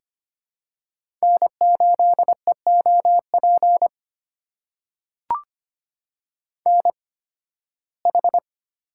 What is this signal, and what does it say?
Morse code, heard as a keyed tone, about 25 wpm, spaced as for 16 wpm, 700 Hz. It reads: N8EOP NH
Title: USA Call Signs - 5 Characters - 8x Character Spacing - 25wpm